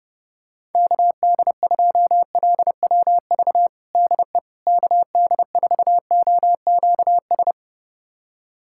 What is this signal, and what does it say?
Morse code, heard as a keyed tone, about 30 wpm, 700 Hz. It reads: KD2LWV DE KD4OQS
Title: USA Callsign DE Callsign - 6 Characters - 30wpm